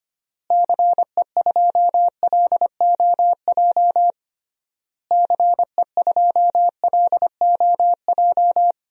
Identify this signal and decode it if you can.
Morse code, heard as a keyed tone, about 25 wpm, 700 Hz. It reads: CE2LOJ CE2LOJ
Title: DXCC-100 Call Signs - 6 Characters - Repeated 3x - 25wpm